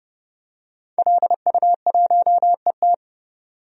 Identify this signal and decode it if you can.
Morse code, heard as a keyed tone, about 30 wpm, 700 Hz. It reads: LU1ET
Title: DXCC-100 Call Signs - 5 Characters - 30wpm